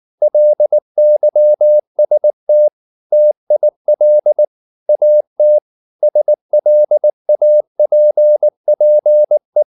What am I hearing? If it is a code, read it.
Morse code, heard as a keyed tone, about 19 wpm, 600 Hz. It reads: LYST TIL AT SLAPPE